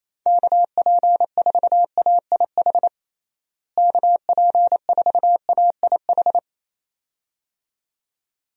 Morse code, heard as a keyed tone, about 28 wpm, 700 Hz. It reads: KP4AIH KP4AIH
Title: DXCC-100 Call Signs - 6 Characters - Repeated 2x - 28wpm